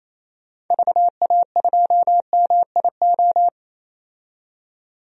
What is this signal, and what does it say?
Morse code, heard as a keyed tone, about 28 wpm, 700 Hz. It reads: VA2MIO
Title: DXCC-100 Call Signs - 6 Characters - 28wpm